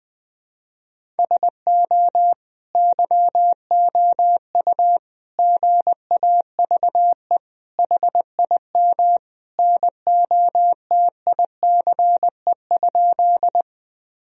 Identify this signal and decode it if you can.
Morse code, heard as a keyed tone, about 20 wpm, 700 Hz. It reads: SO YOU GAVE HIM NOTICE?
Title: Sentences from Top 1000 Words - Review Part 2 - 20wpm